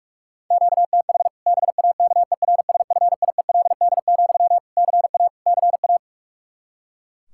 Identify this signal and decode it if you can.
Morse code, heard as a keyed tone, about 41 wpm, 700 Hz. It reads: QTH BAKERSFIELD, CA CA
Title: Practice QSO - Exchange 1 - Beginning - 45wpm